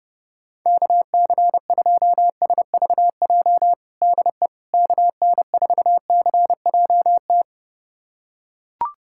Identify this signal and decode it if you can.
Morse code, heard as a keyed tone, about 32 wpm, 700 Hz. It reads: KC2SVJ DE KN4CJT
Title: USA Callsign DE Callsign - 6 Characters - 30wpm